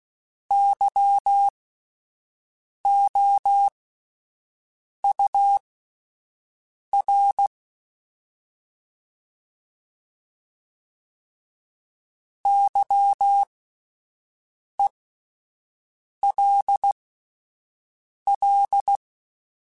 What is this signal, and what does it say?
Morse code, heard as a keyed tone, about 16 wpm, spaced as for 5 wpm, 800 Hz. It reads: YOUR YELL